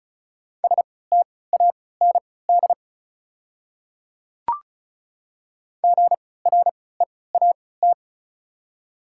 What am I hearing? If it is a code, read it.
Morse code, heard as a keyed tone, about 36 wpm, spaced as for 20 wpm, 690 Hz. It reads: STAND GREAT
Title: 3x Character Spacing - Top 300 Words - 35wpm